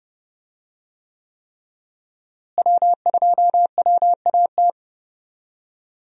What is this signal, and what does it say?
Morse code, heard as a keyed tone, about 31 wpm, 700 Hz. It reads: W2WAT
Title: USA Call Signs - 5 Characters - Repeated 3x - 30wpm